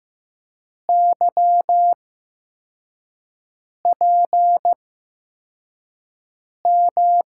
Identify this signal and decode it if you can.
Morse code, heard as a keyed tone, about 15 wpm, spaced as for 4 wpm, 700 Hz. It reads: YPM
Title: USA Call Signs - 6 Characters - 8x Character Spacing - 15wpm